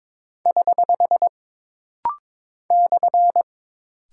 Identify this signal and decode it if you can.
Morse code, heard as a keyed tone, about 22 wpm, 700 Hz. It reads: <HH> /
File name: practice / 22wpm / Single Punctuation-Prosign - Rapid-Fire - 22wpm